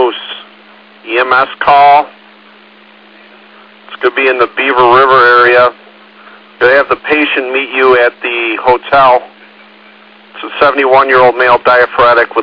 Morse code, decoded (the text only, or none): none